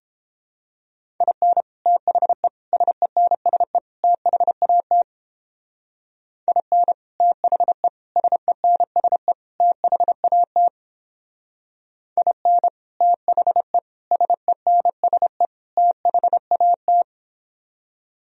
Morse code, heard as a keyed tone, about 31 wpm, 700 Hz. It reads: IN THE SENSE THAT IN THE SENSE THAT IN THE SENSE THAT